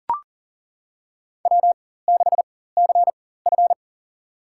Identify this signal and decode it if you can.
Morse code, heard as a keyed tone, about 40 wpm, spaced as for 19 wpm, 700 Hz. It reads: W6CF
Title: USA Call Signs - 4 Characters - 4x Character Spacing - 40wpm